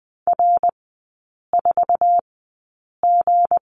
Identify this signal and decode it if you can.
Morse code, heard as a keyed tone, about 20 wpm, spaced as for 8 wpm, 700 Hz. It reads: R4G
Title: 159 Lesson 26e - Letter g - Three Characters 20wpm